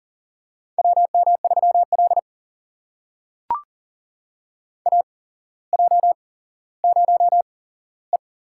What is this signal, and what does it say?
Morse code, heard as a keyed tone, about 41 wpm, spaced as for 17 wpm, 700 Hz. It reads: WM3L AJ0E
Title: USA Call Signs - 4 Characters - 8x Character Spacing - 40wpm